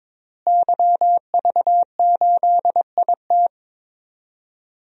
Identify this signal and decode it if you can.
Morse code, heard as a keyed tone, about 21 wpm, 700 Hz. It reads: YV8IT
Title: DXCC-100 Call Signs - 5 Characters - Repeated 3x - 22wpm